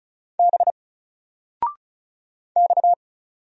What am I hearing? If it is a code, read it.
Morse code, heard as a keyed tone, about 35 wpm, 700 Hz. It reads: B X
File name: Single-Letter-Number-35wpm